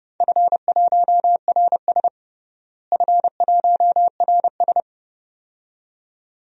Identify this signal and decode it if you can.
Morse code, heard as a keyed tone, about 30 wpm, 700 Hz. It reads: F1RS F1RS
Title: DXCC-100 Call Signs - 4 Characters - Repeated 2x - 30wpm